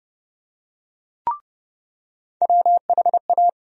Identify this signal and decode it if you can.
Morse code, heard as a keyed tone, about 32 wpm, 700 Hz. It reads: WHA
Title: USA Call Sign Postfixes - 3 Letters 30wpm